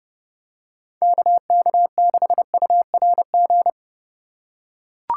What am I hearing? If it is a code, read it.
Morse code, heard as a keyed tone, about 29 wpm, 700 Hz. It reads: KK6URG